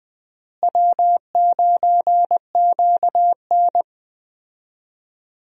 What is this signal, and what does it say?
Morse code, heard as a keyed tone, about 21 wpm, 700 Hz. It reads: W9QN
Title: USA Call Signs - 4 Characters - Repeated 2x - 20wpm